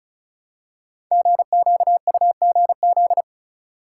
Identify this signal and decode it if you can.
Morse code, heard as a keyed tone, about 36 wpm, 700 Hz. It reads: GQUGZ
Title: Code Groups - 5 Letters - 35wpm